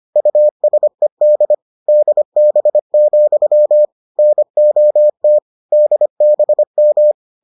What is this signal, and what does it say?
Morse code, heard as a keyed tone, about 25 wpm, 600 Hz. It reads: USED DB, NOT DBM